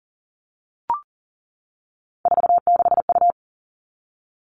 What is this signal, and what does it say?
Morse code, heard as a keyed tone, about 40 wpm, 700 Hz. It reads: 46U
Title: All Cut Numbers - 3-Digit 40wpm